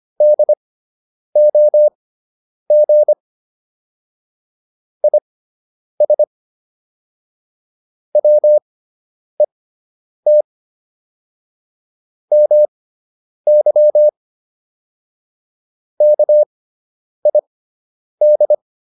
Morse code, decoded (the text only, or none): DOG IS WET MY KID